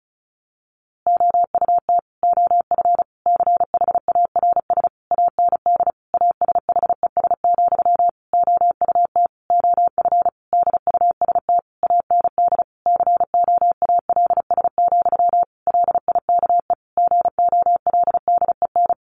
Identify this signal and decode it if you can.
Morse code, heard as a keyed tone, about 34 wpm, 700 Hz. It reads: OUT OF CHARS AND ASHES, OUT OF DUST AND COALS, LIKE GOLDEN
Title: A Sound Of Thunder 35wpm